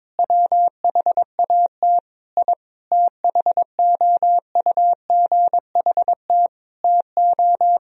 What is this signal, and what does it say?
Morse code, heard as a keyed tone, about 21 wpm, 700 Hz. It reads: WHAT I THOUGHT TO
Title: Sentences from Top 1000 Words - Review Part 2 - 22wpm